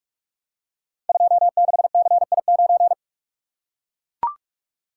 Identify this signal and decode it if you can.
Morse code, heard as a keyed tone, about 44 wpm, 700 Hz. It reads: 26CI9
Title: Code Groups - 5 Letter:Numbers - 45wpm